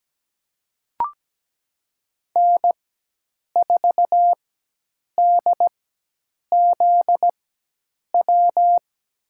Morse code, heard as a keyed tone, about 17 wpm, spaced as for 8 wpm, 700 Hz. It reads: N4DZW
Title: USA Call Signs - 5 Characters - 4x Character Spacing - 17wpm